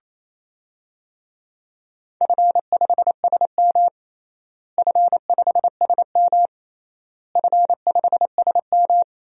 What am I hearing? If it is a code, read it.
Morse code, heard as a keyed tone, about 28 wpm, 700 Hz. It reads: F5SM F5SM F5SM
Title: DXCC-100 Call Signs - 4 Characters - Repeated 3x - 28wpm